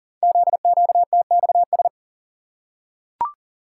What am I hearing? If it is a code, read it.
Morse code, heard as a keyed tone, about 38 wpm, 700 Hz. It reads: ZQTXS
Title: Code Groups - 5 Letters - 40wpm